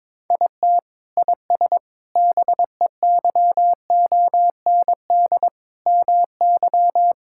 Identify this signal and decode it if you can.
Morse code, heard as a keyed tone, about 22 wpm, 700 Hz. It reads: IT IS BEYOND MY
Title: Sentences from Top 1000 Words - Review Part 6 - 22wpm